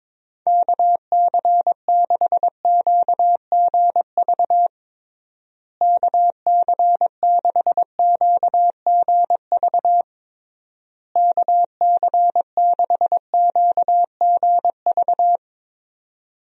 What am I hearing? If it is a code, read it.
Morse code, heard as a keyed tone, about 22 wpm, 700 Hz. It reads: KC6QGV KC6QGV KC6QGV
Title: USA Call Signs - 6 Characters - Repeated 3x - 22wpm